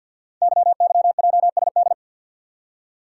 Morse code, heard as a keyed tone, about 48 wpm, 700 Hz. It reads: YYJSD
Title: Code Groups - 5 Letters - 50wpm